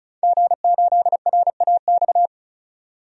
Morse code, heard as a keyed tone, about 34 wpm, 700 Hz. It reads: G8RAX
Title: DXCC-100 Call Signs - 5 Characters - 35wpm